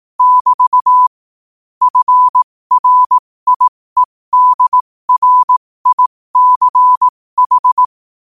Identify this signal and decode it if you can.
Morse code, heard as a keyed tone, about 18 wpm, spaced as for 15 wpm, 1000 Hz. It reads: <BT> FRIEDRICH
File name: qotdfiles / 15wpm